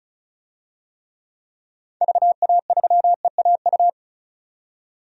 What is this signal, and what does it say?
Morse code, heard as a keyed tone, about 36 wpm, 700 Hz. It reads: VA3EAU
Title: DXCC-100 Call Signs - 6 Characters - 35wpm